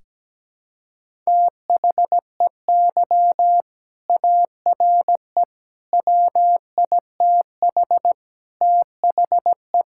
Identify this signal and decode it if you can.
Morse code, heard as a keyed tone, about 17 wpm, 700 Hz. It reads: THEY ARE WITH THE
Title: Sentences from Top 1000 Words - Review Part 5 - 17wpm